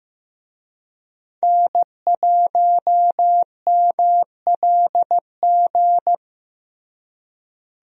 Morse code, heard as a keyed tone, about 15 wpm, 700 Hz. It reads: N1MLG
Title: USA Call Signs - 5 Characters - Repeated 2x - 15wpm